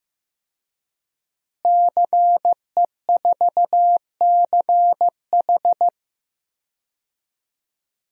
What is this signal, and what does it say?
Morse code, heard as a keyed tone, about 15 wpm, 700 Hz. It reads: CE4CH
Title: DXCC-100 Call Signs - 5 Characters - Repeated 2x - 15wpm